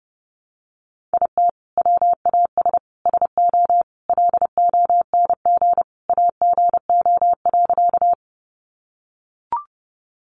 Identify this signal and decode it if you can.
Morse code, heard as a keyed tone, about 30 wpm, 700 Hz. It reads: IT WAS SO LONG AGO.
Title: Sentences from Top 500 Words 30wpm